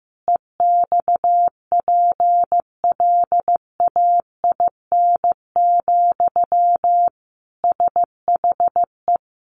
Morse code, heard as a keyed tone, about 15 wpm, 700 Hz. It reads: EXPLAIN, SHE
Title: Sentences from Top 1100 Words 15wpm